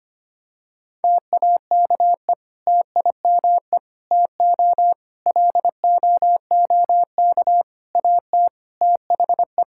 Morse code, heard as a keyed tone, about 25 wpm, 700 Hz. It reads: TAKE TIME TO LOOK AT THE
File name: practice / 25wpm / Sentences with 4-Word Phrases - 25wpm